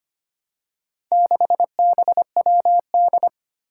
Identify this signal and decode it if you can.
Morse code, heard as a keyed tone, about 25 wpm, 700 Hz. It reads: 6BWD
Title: Four Challenging Letter-Number Combinations 25wpm